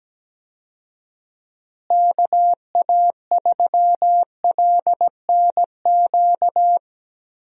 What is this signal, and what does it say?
Morse code, heard as a keyed tone, about 17 wpm, 700 Hz. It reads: KA3LNQ
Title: USA Call Signs - 6 Characters - Repeated 2x - 17wpm